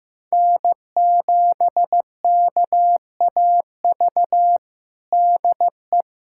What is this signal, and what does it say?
Morse code, heard as a keyed tone, about 16 wpm, 700 Hz. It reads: N7KAV DE
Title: USA Callsign DE Callsign - 5 Characters - 15wpm